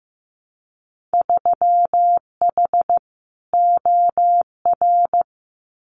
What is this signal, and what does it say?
Morse code, heard as a keyed tone, about 15 wpm, 700 Hz. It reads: 3H OR
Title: Winter Field Day Exchanges Encore 15wpm